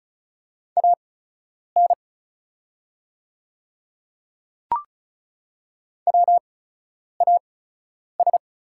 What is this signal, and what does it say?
Morse code, heard as a keyed tone, about 36 wpm, spaced as for 10 wpm, 700 Hz. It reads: AN WAS
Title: 8x Character Spacing - Top 100 Words - 35wpm